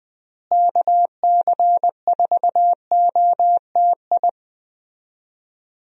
Morse code, heard as a keyed tone, about 21 wpm, 700 Hz. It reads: KC4OTI
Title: USA Call Signs - 6 Characters - Repeated 2x - 20wpm